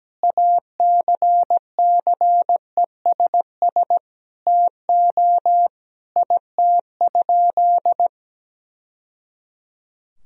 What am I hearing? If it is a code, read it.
Morse code, heard as a keyed tone, about 17 wpm, 700 Hz. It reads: ACCESS TO IT?
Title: Sentences from Top 1000 Words - Review Part 4 - 17wpm